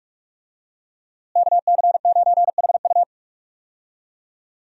Morse code, heard as a keyed tone, about 45 wpm, 700 Hz. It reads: KC9HU
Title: USA Call Signs - 5 Characters - Repeated 3x - 45wpm